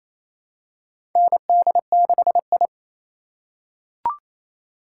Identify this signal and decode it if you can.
Morse code, heard as a keyed tone, about 27 wpm, 700 Hz. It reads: ND6I
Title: USA Call Signs - 4 Characters - 3x Character Spacing - 28wpm